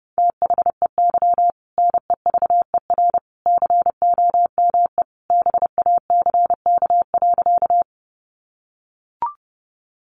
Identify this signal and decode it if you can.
Morse code, heard as a keyed tone, about 30 wpm, 700 Hz. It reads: THEY NEVER COME BACK.
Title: Sentences from Top 300 Words 30wpm